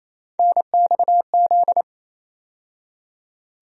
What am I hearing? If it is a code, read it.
Morse code, heard as a keyed tone, about 28 wpm, 700 Hz. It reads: NXZ
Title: USA Call Sign Postfixes - 3 Letters 28wpm